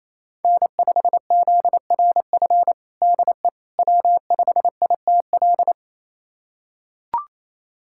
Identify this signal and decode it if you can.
Morse code, heard as a keyed tone, about 28 wpm, 700 Hz. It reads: N5ZRF DE W5ITL
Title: USA Callsign DE Callsign - 5 Characters - 28wpm